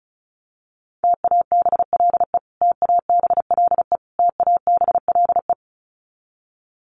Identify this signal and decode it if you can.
Morse code, heard as a keyed tone, about 34 wpm, 700 Hz. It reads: TABLE TABLE TABLE
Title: Top 500 Words Repeated 35wpm